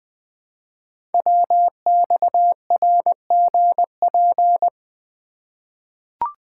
Morse code, heard as a keyed tone, about 20 wpm, 700 Hz. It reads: WXRGP
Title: Code Groups - 5 Letters - 20wpm